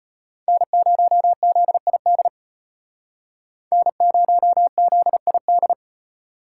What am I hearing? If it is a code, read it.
Morse code, heard as a keyed tone, about 32 wpm, 700 Hz. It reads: N0ZID N0ZID